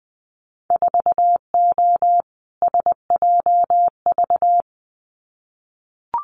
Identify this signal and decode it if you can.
Morse code, heard as a keyed tone, about 20 wpm, 700 Hz. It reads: 4O SJV
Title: Winter Field Day Exchanges Encore 20wpm